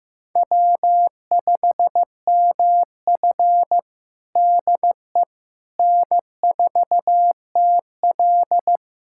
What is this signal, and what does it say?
Morse code, heard as a keyed tone, about 15 wpm, 700 Hz. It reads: W5MF DE N4TL
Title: USA Callsign DE Callsign - 4 Characters - 15wpm